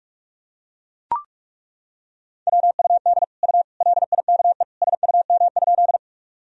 Morse code, heard as a keyed tone, about 41 wpm, 700 Hz. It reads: WUD U LIKE SUM?
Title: Sentences from Top 100 Words and Abbreviations 45wpm